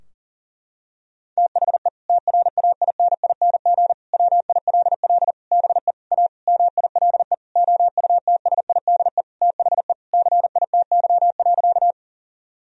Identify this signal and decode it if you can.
Morse code, heard as a keyed tone, about 40 wpm, 700 Hz. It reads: THE TRAINING WILL BE A MILE OUTSIDE THE CITY.
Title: Sentences from Top 1000 Words - Review Part 4 - 40wpm